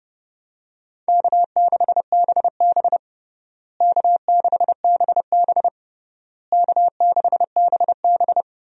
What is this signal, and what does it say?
Morse code, heard as a keyed tone, about 30 wpm, 700 Hz. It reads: K6BB K6BB K6BB
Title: USA Call Signs - 4 Characters - Repeated 3x - 30wpm